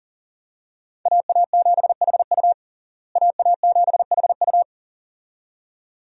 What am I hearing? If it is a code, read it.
Morse code, heard as a keyed tone, about 40 wpm, 700 Hz. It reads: AA7HU AA7HU
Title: USA Call Signs - 5 Characters - Repeated 2x - 40wpm